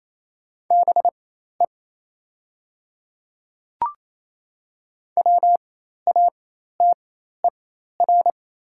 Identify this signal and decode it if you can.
Morse code, heard as a keyed tone, about 28 wpm, spaced as for 13 wpm, 690 Hz. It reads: BE WATER
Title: 4x Character Spacing - Top 200 Words - 28wpm